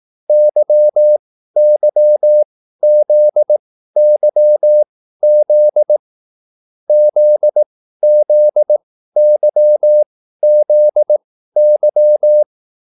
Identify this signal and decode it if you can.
Morse code, heard as a keyed tone, about 18 wpm, spaced as for 13 wpm, 600 Hz. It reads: YYZYZ ZZYZY